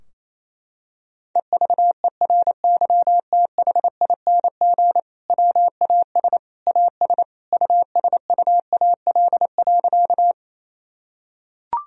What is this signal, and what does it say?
Morse code, heard as a keyed tone, about 27 wpm, 700 Hz. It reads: EVERYTHING WAS AS USUAL.
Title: Sentences from Top 1000 Words - Review Part 2 - 28wpm